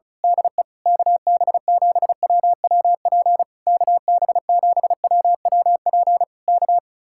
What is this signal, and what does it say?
Morse code, heard as a keyed tone, about 33 wpm, 700 Hz. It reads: DE KB7WWP KB7WWP K
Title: Speed-Racing Common QSO Phrases Encore - 35wpm